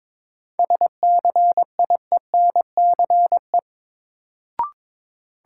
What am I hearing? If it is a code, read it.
Morse code, heard as a keyed tone, about 21 wpm, 700 Hz. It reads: SCIENCE
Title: Sentences from Top 1000 Words - Review Part 2 - New Word Drill - 22wpm